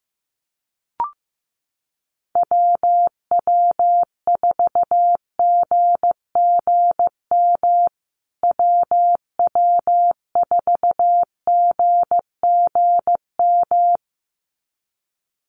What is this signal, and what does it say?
Morse code, heard as a keyed tone, about 15 wpm, 700 Hz. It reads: WW4GGM WW4GGM
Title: USA Call Signs Repeated 2x 15wpm